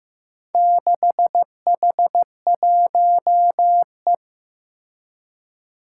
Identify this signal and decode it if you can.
Morse code, heard as a keyed tone, about 15 wpm, 700 Hz. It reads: 6H1E
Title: Four Challenging Letter-Number Combinations 15wpm